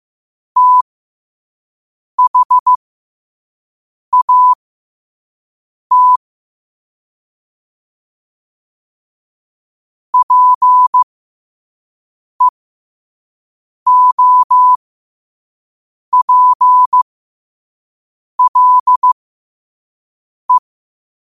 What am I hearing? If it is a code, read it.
Morse code, heard as a keyed tone, about 15 wpm, spaced as for 5 wpm, 1000 Hz. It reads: THAT PEOPLE
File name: qotdfiles / 5wpm